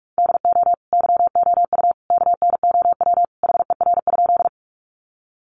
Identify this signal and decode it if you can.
Morse code, heard as a keyed tone, about 44 wpm, 700 Hz. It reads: DO YOU KNOW HER?
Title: Sentences from Top 100 Words 45wpm